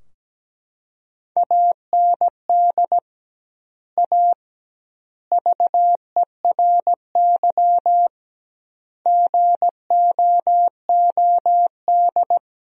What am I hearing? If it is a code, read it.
Morse code, heard as a keyed tone, about 17 wpm, 700 Hz. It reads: AND A VERY GOOD
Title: 2x Word-Spacing - Sentences from Top 200 Words - 17wpm